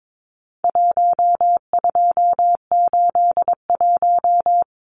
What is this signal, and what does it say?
Morse code, heard as a keyed tone, about 22 wpm, 700 Hz. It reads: 1281
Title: Numbers - 4-Digit 22wpm